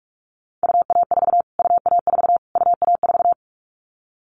Morse code, heard as a keyed tone, about 44 wpm, 700 Hz. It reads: UA4 UA4 UA4
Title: Callsign Prefixes for Top 25 DXCC - Repeated 3x - 45wpm